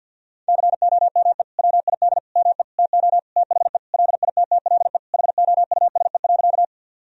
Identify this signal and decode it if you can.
Morse code, heard as a keyed tone, about 50 wpm, 700 Hz. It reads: COME WID ME TO THE LITTLE HOUSE.
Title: Sentences from Top 200 Words and Abbreviations 50wpm